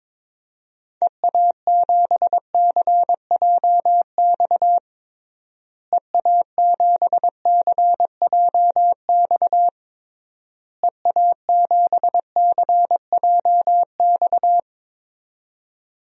Morse code, heard as a keyed tone, about 22 wpm, 700 Hz. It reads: EA7CJX EA7CJX EA7CJX
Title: DXCC-100 Call Signs - 6 Characters - Repeated 3x - 22wpm